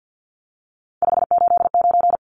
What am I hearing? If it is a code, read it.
Morse code, heard as a keyed tone, about 50 wpm, 700 Hz. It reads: 589
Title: Numbers - 3-Digit 50wpm